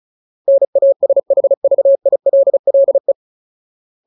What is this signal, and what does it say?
Morse code, heard as a keyed tone, about 34 wpm, 550 Hz. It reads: NASHVILLE
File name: Ultimate Word List / 35WPM Ultimate